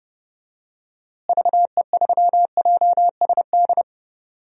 Morse code, heard as a keyed tone, about 31 wpm, 700 Hz. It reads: VE3JSD
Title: DXCC-100 Call Signs - 6 Characters - Repeated 3x - 30wpm